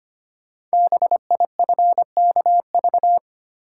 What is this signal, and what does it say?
Morse code, heard as a keyed tone, about 25 wpm, 700 Hz. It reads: BIFKV
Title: Code Groups - 5 Letter:Numbers - 25wpm